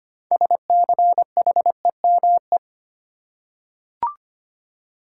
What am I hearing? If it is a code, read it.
Morse code, heard as a keyed tone, about 25 wpm, 700 Hz. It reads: SCHEME
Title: Six Letter Words - 25wpm